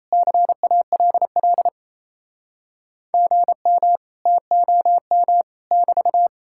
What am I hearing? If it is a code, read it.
Morse code, heard as a keyed tone, about 37 wpm, 700 Hz. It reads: CALL GM TOM <BT>